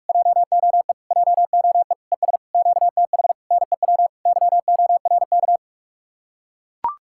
Morse code, heard as a keyed tone, about 44 wpm, 700 Hz. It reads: JOE JOE ES QTH NEW YORK